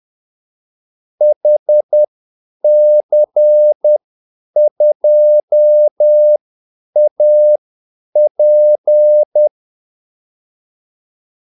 Morse code, heard as a keyed tone, about 10 wpm, spaced as for 8 wpm, 600 Hz. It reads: HC2AP
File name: Callsigns-10-8wpm